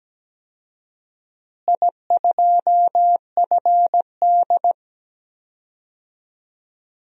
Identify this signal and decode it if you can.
Morse code, heard as a keyed tone, about 17 wpm, 700 Hz. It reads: I2FD
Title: DXCC-100 Call Signs - 4 Characters - Repeated 3x - 17wpm